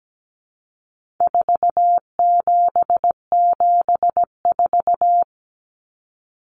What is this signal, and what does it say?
Morse code, heard as a keyed tone, about 17 wpm, 700 Hz. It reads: 4774
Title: Common Cut Numbers - 4-Digit 17wpm